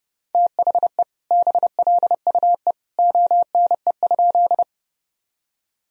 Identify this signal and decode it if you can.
Morse code, heard as a keyed tone, about 30 wpm, 700 Hz. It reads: THE BLUE ONE?
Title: Sentences from Top 1000 Words - Review Part 3 - 30wpm